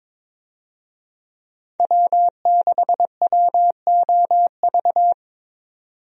Morse code, heard as a keyed tone, about 22 wpm, 700 Hz. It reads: W6WOV
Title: USA Call Signs - 5 Characters - 22wpm